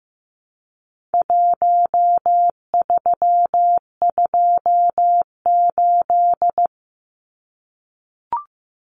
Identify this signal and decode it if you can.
Morse code, heard as a keyed tone, about 15 wpm, 700 Hz. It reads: 1328
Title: Numbers - 4-Digit 15wpm